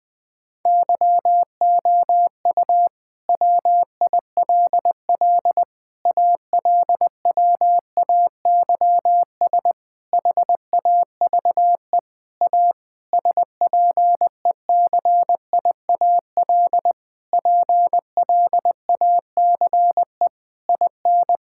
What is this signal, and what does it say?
Morse code, heard as a keyed tone, about 20 wpm, 700 Hz. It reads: YOU WILL ALWAYS HAVE A SPECIAL PLACE IN
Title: Sentences with 9-Word Phrases - 20wpm